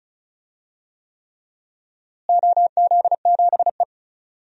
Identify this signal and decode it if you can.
Morse code, heard as a keyed tone, about 36 wpm, 700 Hz. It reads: OZ7E
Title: DXCC-100 Call Signs - 4 Characters - Repeated 3x - 35wpm